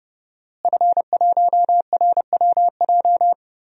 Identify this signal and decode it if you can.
Morse code, heard as a keyed tone, about 31 wpm, 700 Hz. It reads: F1RWJ
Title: DXCC-100 Call Signs - 5 Characters - Repeated 3x - 30wpm